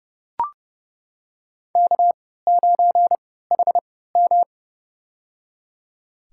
Morse code, heard as a keyed tone, about 30 wpm, spaced as for 17 wpm, 700 Hz. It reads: K9HM
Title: USA Call Signs - 4 Characters - 3x Character Spacing - 30wpm